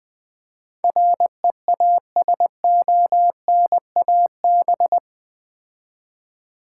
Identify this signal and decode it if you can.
Morse code, heard as a keyed tone, about 21 wpm, 700 Hz. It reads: REASONAB
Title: Build A Word - 4 Letters At A Time - 10-Letter Words - 20wpm